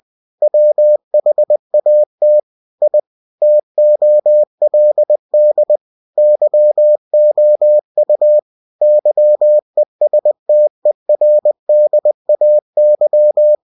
Morse code, heard as a keyed tone, about 20 wpm, 600 Hz. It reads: WHAT I TOLD YOU YESTERDAY
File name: cw-books / winnie-the-pooh / winnie-20wpm-page-0072